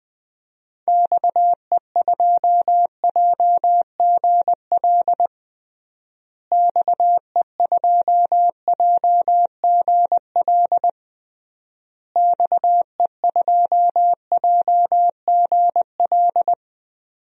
Morse code, heard as a keyed tone, about 21 wpm, 700 Hz. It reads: XE2JGL XE2JGL XE2JGL
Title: DXCC-100 Call Signs - 6 Characters - Repeated 3x - 20wpm